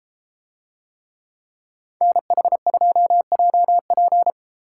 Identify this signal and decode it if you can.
Morse code, heard as a keyed tone, about 31 wpm, 700 Hz. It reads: NH2JP